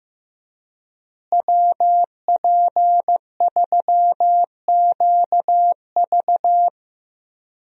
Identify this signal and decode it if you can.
Morse code, heard as a keyed tone, about 15 wpm, 700 Hz. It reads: WP3QV
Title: USA Call Signs - 5 Characters - Repeated 3x - 15wpm